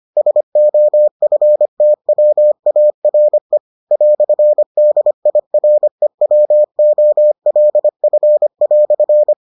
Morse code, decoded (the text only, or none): SOFTWARE "DIREWOLF"